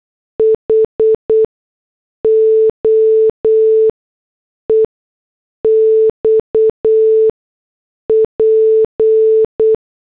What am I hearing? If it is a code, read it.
Morse code, heard as a keyed tone, about 8 wpm, spaced as for 6 wpm, 440 Hz. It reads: HO EXP